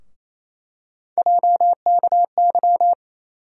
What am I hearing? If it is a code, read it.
Morse code, heard as a keyed tone, about 28 wpm, 700 Hz. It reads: JKY
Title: USA Call Sign Postfixes - 3 Letters 28wpm